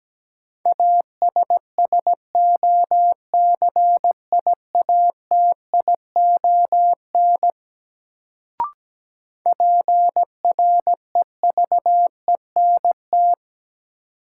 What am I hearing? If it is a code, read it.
Morse code, heard as a keyed tone, about 17 wpm, 700 Hz. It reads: ASSOCIATION PREVENT
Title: Sentences from Top 1100 Words - New Word Drill - 17wpm